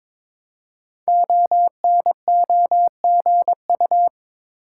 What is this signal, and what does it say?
Morse code, heard as a keyed tone, about 22 wpm, 700 Hz. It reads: ONOGU
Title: Code Groups - 5 Letters - 22wpm